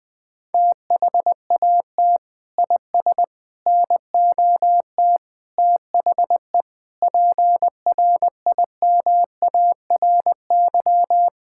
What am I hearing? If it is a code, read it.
Morse code, heard as a keyed tone, about 20 wpm, 700 Hz. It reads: THAT IS NOT THE PRIMARY
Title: Sentences from Top 1000 Words - Review Part 5 - 20wpm